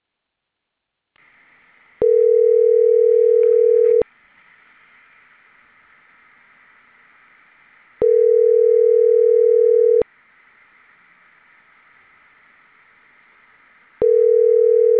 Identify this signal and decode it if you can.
Ringback tone (North America), 440 + 480 Hz.